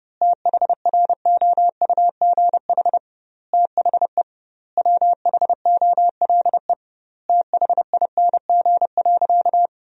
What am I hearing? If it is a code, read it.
Morse code, heard as a keyed tone, about 32 wpm, 700 Hz. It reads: THROUGH THE WHOLE THING.
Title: 2x Word-Spacing - Sentences from Top 400 Words - 30wpm